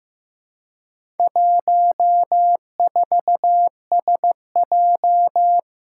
Morse code, heard as a keyed tone, about 15 wpm, 700 Hz. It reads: 14SJ
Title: Four Challenging Letter-Number Combinations 15wpm